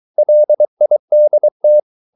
Morse code, heard as a keyed tone, about 23 wpm, 600 Hz. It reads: LIDT
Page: Ferie 23 wpm